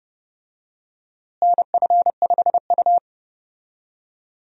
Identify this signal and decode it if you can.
Morse code, heard as a keyed tone, about 31 wpm, 700 Hz. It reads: NF5U
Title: USA Call Signs - 4 Characters - Repeated 2x - 30wpm